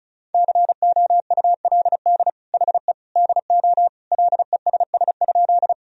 Morse code, heard as a keyed tone, about 36 wpm, 700 Hz. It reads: COULD HE DO LESS?
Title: Sentences from Top 1000 Words - Review Part 2 - 35wpm